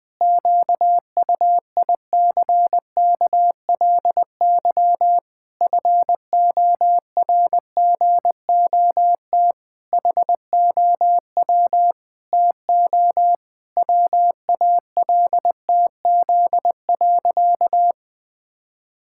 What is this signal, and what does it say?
Morse code, heard as a keyed tone, about 21 wpm, 700 Hz. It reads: QUICKLY FORGOT HOW TO WALTZ.
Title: Pangram Challenge v2 - 20wpm